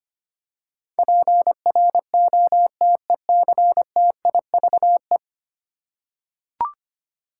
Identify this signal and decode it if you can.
Morse code, heard as a keyed tone, about 25 wpm, 700 Hz. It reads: PROTECTIVE
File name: Ten Letter Words - 25wpm